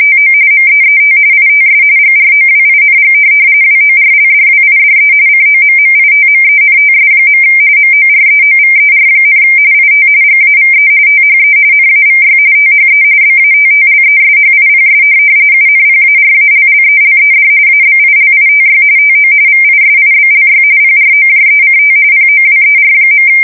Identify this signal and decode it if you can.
RTTY, about 45.45 baud. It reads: TO MEET ITS GROWING WORLDWIDE COMMITMENTS. MOREOVER, IT GAVE THE NAVY THE BATTLESHIPS UNPARALLELED ABILITY TO VISIBLY PROJECT POWER.
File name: fsk170 / USS-New-Jersey